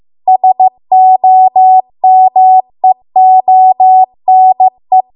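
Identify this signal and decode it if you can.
Morse code, heard as a keyed tone, about 15 wpm, 750 Hz. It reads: SOMEONE
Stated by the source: Morse code practice at 15 wpm from the national amateur radio association's code archive.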